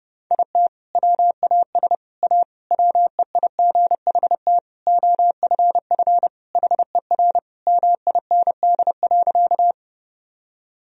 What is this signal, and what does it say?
Morse code, heard as a keyed tone, about 30 wpm, 700 Hz. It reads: IT WAS A WEIGHT OFF HER MIND.
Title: Sentences from Top 1000 Words - Review Part 3 - 30wpm